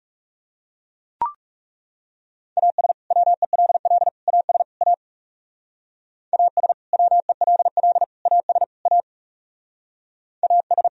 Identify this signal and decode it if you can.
Morse code, heard as a keyed tone, about 38 wpm, 700 Hz. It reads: AS WELL AS A AS WELL AS A AS